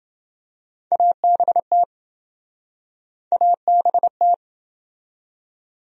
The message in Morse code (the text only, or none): ABT ABT